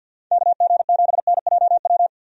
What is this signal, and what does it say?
Morse code, heard as a keyed tone, about 47 wpm, 700 Hz. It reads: KG7NJW
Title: USA Call Signs - 6 Characters - 50wpm